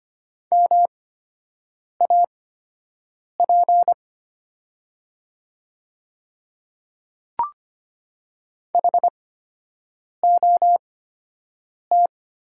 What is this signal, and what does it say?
Morse code, heard as a keyed tone, about 25 wpm, spaced as for 7 wpm, 700 Hz. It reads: MAP HOT
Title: 8x Character Spacing - Top 500 Words - 25wpm